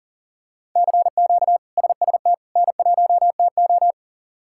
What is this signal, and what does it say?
Morse code, heard as a keyed tone, about 40 wpm, 700 Hz. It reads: CQ SST N1TO
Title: K1USN SST QSO Practice - 40wpm